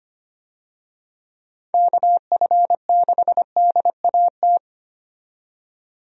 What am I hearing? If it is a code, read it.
Morse code, heard as a keyed tone, about 25 wpm, 700 Hz. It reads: KF6DAT
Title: USA Call Signs - 6 Characters - 25wpm